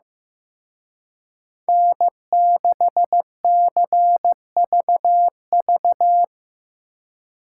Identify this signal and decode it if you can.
Morse code, heard as a keyed tone, about 18 wpm, 700 Hz. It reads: N6CVV
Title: Speed Racing - USA Call Signs 15wpm